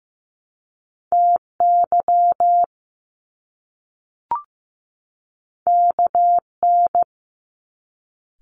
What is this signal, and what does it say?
Morse code, heard as a keyed tone, about 15 wpm, 700 Hz. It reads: TY KN
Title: Top Two Letter Combinations in Words 15wpm